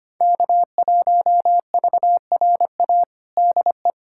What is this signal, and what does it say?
Morse code, heard as a keyed tone, about 25 wpm, 700 Hz. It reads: K1VRA DE
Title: USA Callsign DE Callsign - 5 Characters - 25wpm